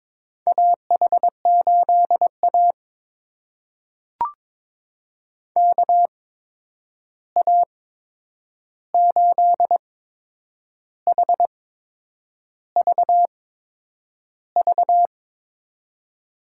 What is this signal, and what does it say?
Morse code, heard as a keyed tone, about 22 wpm, spaced as for 8 wpm, 700 Hz. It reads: AH8A KA8HVV
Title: USA Call Signs - 8x Character Spacing - 22wpm